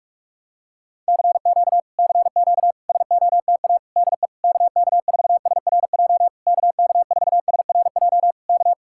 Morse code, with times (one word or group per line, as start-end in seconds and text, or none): 1.1-1.8 CQ
2.0-2.7 CQ
2.9-3.8 SOTA
4.0-4.3 DE
4.4-6.3 KK4SRJ
6.5-8.3 KK4SRJ
8.5-8.7 K